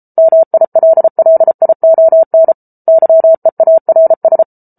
Morse code, heard as a keyed tone, about 34 wpm, 650 Hz. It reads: MILLION YEARS